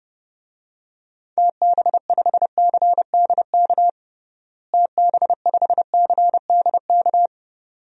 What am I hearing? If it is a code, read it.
Morse code, heard as a keyed tone, about 30 wpm, 700 Hz. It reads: TB5CDK TB5CDK
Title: DXCC-100 Call Signs - 6 Characters - Repeated 2x - 30wpm